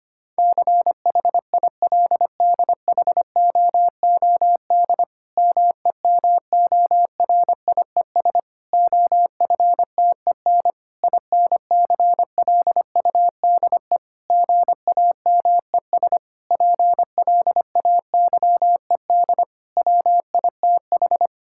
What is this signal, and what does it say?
Morse code, heard as a keyed tone, about 25 wpm, 700 Hz. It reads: CHILDHOOD MEMORIES OFTEN INCLUDE GAMES PLAYED WITH
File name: practice / 25wpm / Difficult Sentences with Hints - 25wpm